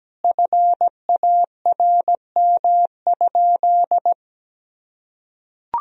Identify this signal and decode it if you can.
Morse code, heard as a keyed tone, about 17 wpm, 700 Hz. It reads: FARM?
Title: Sentences from Top 1000 Words - Review Part 3 - 17wpm